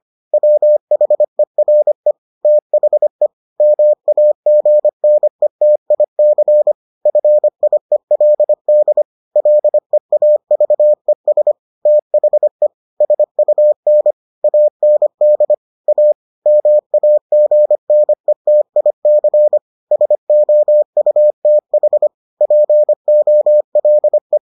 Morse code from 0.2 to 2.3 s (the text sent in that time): WHERE